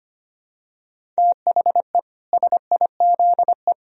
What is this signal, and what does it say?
Morse code, heard as a keyed tone, about 24 wpm, 700 Hz. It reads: THE SIZE
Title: Speed-Racing - Sets of 4 Words 25wpm